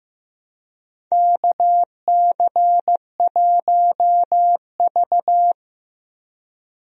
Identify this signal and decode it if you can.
Morse code, heard as a keyed tone, about 15 wpm, 700 Hz. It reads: KC1V
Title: USA Call Signs - 4 Characters - Repeated 2x - 15wpm